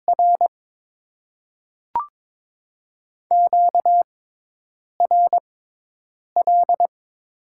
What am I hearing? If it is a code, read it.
Morse code, heard as a keyed tone, about 22 wpm, spaced as for 8 wpm, 700 Hz. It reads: R QRL
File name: Top 20 QSO Elements - 6x Character Spacing - 22wpm